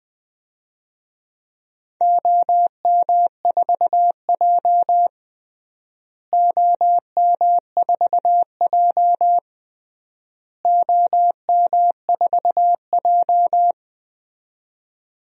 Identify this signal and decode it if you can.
Morse code, heard as a keyed tone, about 20 wpm, 700 Hz. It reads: OM4J OM4J OM4J
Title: DXCC-100 Call Signs - 4 Characters - Repeated 3x - 20wpm